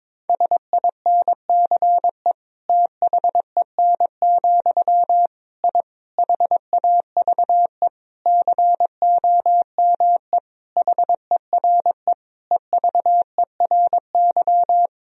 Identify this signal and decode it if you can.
Morse code, heard as a keyed tone, about 22 wpm, 700 Hz. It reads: SINCE THEN, I HAVE COME HERE EVERY
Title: Sentences from Top 1000 Words - Review Part 1 - 22wpm